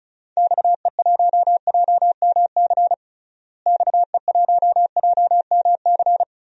Morse code, heard as a keyed tone, about 34 wpm, 700 Hz. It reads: XE1JMC XE1JMC
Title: DXCC-100 Call Signs - 6 Characters - Repeated 2x - 35wpm